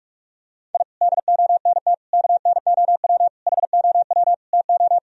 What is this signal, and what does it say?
Morse code, heard as a keyed tone, about 46 wpm, 700 Hz. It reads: I DONT KNOW HOW TO
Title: Sentences from Top 1000 Words - Review Part 1 - 45wpm